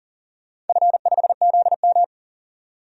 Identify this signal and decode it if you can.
Morse code, heard as a keyed tone, about 37 wpm, 700 Hz. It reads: F5ZM
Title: DXCC-100 Call Signs - 4 Characters - 40wpm